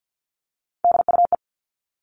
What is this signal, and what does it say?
Morse code, heard as a keyed tone, about 50 wpm, 700 Hz. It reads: DUE
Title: Top 751-1000 Words 50wpm